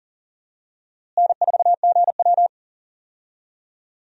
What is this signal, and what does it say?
Morse code, heard as a keyed tone, about 37 wpm, 700 Hz. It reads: N4GW
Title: USA Call Signs - 4 Characters - Repeated 2x - 40wpm